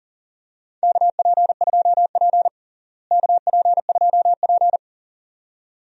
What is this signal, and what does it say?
Morse code, heard as a keyed tone, about 40 wpm, 700 Hz. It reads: KP2P KP2P
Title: DXCC-100 Call Signs - 4 Characters - Repeated 2x - 40wpm